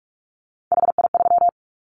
Morse code, heard as a keyed tone, about 44 wpm, 700 Hz. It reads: HI3
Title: Callsign Prefixes for Top 100 DXCC - 45wpm